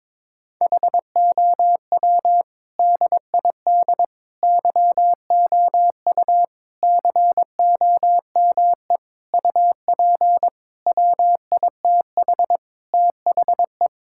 Morse code, heard as a keyed tone, about 22 wpm, 700 Hz. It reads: HOW DID YOU COME UP WITH THE
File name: practice / 22wpm / Sets of 7 Words - 22wpm